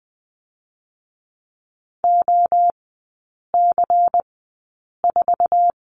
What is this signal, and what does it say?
Morse code, heard as a keyed tone, about 20 wpm, spaced as for 8 wpm, 700 Hz. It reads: OC4
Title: 218 Lesson 35e - Number 0 - Three Characters 20wpm